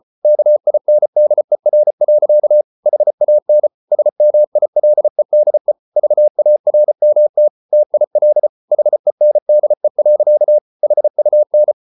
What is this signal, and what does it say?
Morse code, heard as a keyed tone, about 34 wpm, 600 Hz. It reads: KINDER. HAN SMILEDE VARMT TIL HENDE. HUN
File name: Ferie QRQ 34wpm